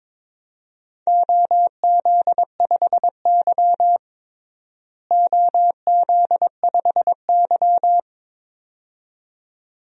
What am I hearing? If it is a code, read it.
Morse code, heard as a keyed tone, about 22 wpm, 700 Hz. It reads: OZ5Y OZ5Y
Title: DXCC-100 Call Signs - 4 Characters - Repeated 2x - 22wpm